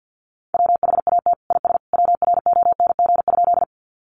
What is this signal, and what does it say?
Morse code, heard as a keyed tone, about 49 wpm, 700 Hz. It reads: WHAT IS WRONG?
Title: Sentences from Top 900 Words 50wpm